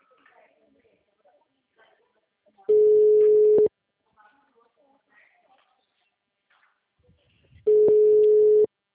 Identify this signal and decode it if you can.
Ringback tone (Europe, 425 Hz), with speech.
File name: out- -2025- - - .73121